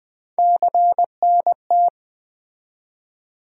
Morse code, heard as a keyed tone, about 20 wpm, 700 Hz. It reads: CNT
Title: USA Call Sign Postfixes - 3 Letters 20wpm